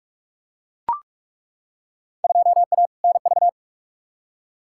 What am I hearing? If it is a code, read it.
Morse code, heard as a keyed tone, about 41 wpm, 700 Hz. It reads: 2A NV
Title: Field Day Exchanges - 45wpm